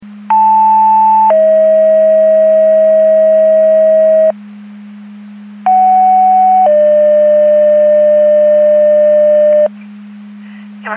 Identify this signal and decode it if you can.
Two-tone page, heard as a one-second tone, then a three-second tone, about 880 then 640 Hz, about 750 then 600 Hz. It is heard from 0.3 to 9.7 s.
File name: Portage 154.385 Paging